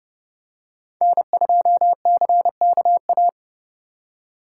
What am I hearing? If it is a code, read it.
Morse code, heard as a keyed tone, about 29 wpm, 700 Hz. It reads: N2CKA